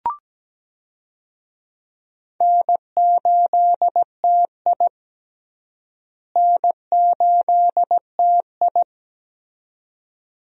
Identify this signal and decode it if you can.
Morse code, heard as a keyed tone, about 17 wpm, 700 Hz. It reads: N8TI N8TI
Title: USA Call Signs - 4 Characters - Repeated 2x - 17wpm